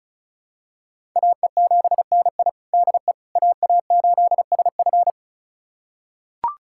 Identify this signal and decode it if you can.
Morse code, heard as a keyed tone, about 36 wpm, 700 Hz. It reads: AE7NI DE AA8SF
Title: USA Callsign DE Callsign - 5 Characters - 35wpm